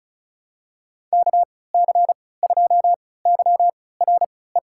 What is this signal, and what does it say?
Morse code, heard as a keyed tone, about 34 wpm, spaced as for 20 wpm, 700 Hz. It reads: KC2YRE
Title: USA Call Signs - 6 Characters - 3x Character Spacing - 35wpm